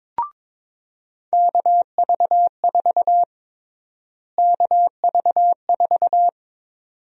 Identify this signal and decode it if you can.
Morse code, heard as a keyed tone, about 22 wpm, 700 Hz. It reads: KV4 KV4
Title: USA Call Sign Prefixes - Repeated 2x - 22wpm